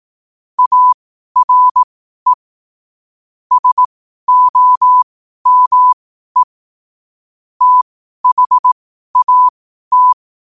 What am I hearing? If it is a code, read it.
Morse code, heard as a keyed tone, about 18 wpm, spaced as for 12 wpm, 1000 Hz. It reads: ARE SOME THAT
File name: qotdfiles / 12wpm